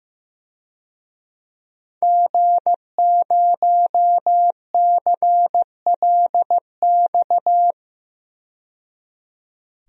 Morse code, heard as a keyed tone, about 15 wpm, 700 Hz. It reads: G0CLX
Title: DXCC-100 Call Signs - 5 Characters - 15wpm